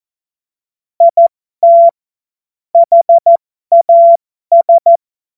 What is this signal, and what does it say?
Morse code, heard as a keyed tone, about 14 wpm, spaced as for 12 wpm, 680 Hz. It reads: IT HAS
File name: Morsekurs / Witze